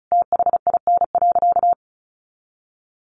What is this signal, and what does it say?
Morse code, heard as a keyed tone, about 34 wpm, 700 Hz. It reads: THIN.
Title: Sentences from Top 1200 Words 35wpm